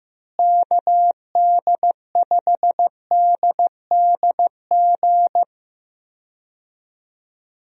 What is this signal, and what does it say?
Morse code, heard as a keyed tone, about 16 wpm, 700 Hz. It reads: KD5DDG
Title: USA Call Signs - 6 Characters - Repeated 2x - 15wpm